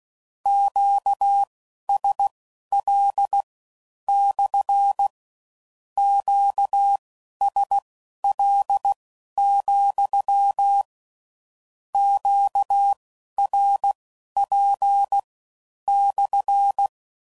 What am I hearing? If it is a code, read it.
Morse code, heard as a keyed tone, about 16 wpm, spaced as for 11 wpm, 800 Hz. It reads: QSL / QSL, QRP /
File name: CW Code Groups @15WPM / Q Calls / Q Calls SPSL2